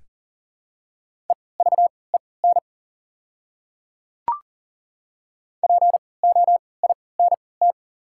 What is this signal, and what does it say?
Morse code, heard as a keyed tone, about 39 wpm, spaced as for 22 wpm, 690 Hz. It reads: EVEN POINT
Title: 3x Character Spacing - Top 300 Words - 40wpm